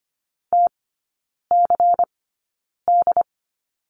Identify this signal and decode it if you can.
Morse code, heard as a keyed tone, about 25 wpm, spaced as for 9 wpm, 700 Hz. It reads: TCD
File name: course / 25:10wpm / 093 Lesson 16e - Letter c - Three Characters 25wpm